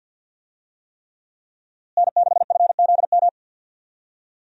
Morse code, heard as a keyed tone, about 53 wpm, 700 Hz. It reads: N6FZM
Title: USA Call Signs - 5 Characters - Repeated 3x - 50wpm